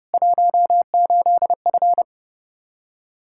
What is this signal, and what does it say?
Morse code, heard as a keyed tone, about 32 wpm, 700 Hz. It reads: 18F
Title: Practice QSO - Exchange 2 - Info - 30wpm